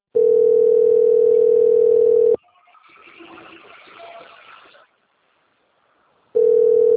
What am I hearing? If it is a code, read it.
Ringback tone (North America), 440 + 480 Hz.